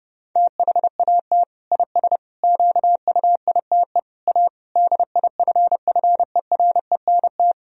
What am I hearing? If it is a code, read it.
Morse code, heard as a keyed tone, about 30 wpm, 700 Hz. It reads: THAT IS QUITE A DIFFERENT
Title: Sentences from Top 1000 Words - Review Part 3 - 30wpm